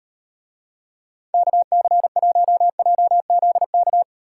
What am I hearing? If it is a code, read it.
Morse code, heard as a keyed tone, about 38 wpm, 700 Hz. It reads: KC1JZK